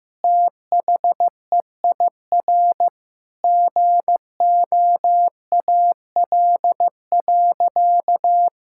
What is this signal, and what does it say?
Morse code, heard as a keyed tone, about 15 wpm, 700 Hz. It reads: THEIR GOAL.
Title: Sentences from Top 1000 Words - Review Part 6 - 15wpm